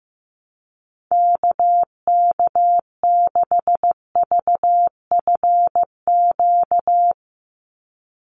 Morse code, heard as a keyed tone, about 15 wpm, 700 Hz. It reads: KK6VFQ
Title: USA Call Signs 15wpm